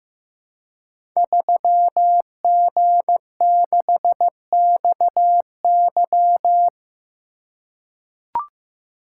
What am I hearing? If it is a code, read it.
Morse code, heard as a keyed tone, about 15 wpm, 700 Hz. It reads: 3G6XY
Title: Code Groups - 5 Letter:Numbers - 15wpm